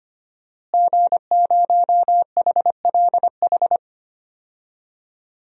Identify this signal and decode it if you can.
Morse code, heard as a keyed tone, about 25 wpm, 700 Hz. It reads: G0HLH
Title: DXCC-100 Call Signs - 5 Characters - Repeated 3x - 25wpm